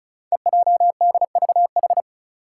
Morse code, heard as a keyed tone, about 36 wpm, 700 Hz. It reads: EJDVH
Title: Five Challenging Letter-Number Combinations 35wpm